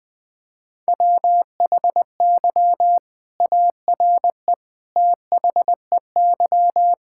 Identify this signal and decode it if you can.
Morse code, heard as a keyed tone, about 20 wpm, 700 Hz. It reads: WHY ARE THEY
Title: Sentences from Top 1000 Words - Review Part 5 - 20wpm